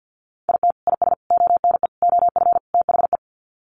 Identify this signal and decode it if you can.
Morse code, heard as a keyed tone, about 48 wpm, 700 Hz. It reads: IT IS ONE OF THE
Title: Sets of 5 Words 50wpm